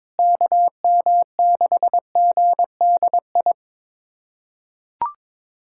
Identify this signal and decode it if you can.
Morse code, heard as a keyed tone, about 22 wpm, 700 Hz. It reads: KM6GDI
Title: USA Callsign DE Callsign - 6 Characters - 22wpm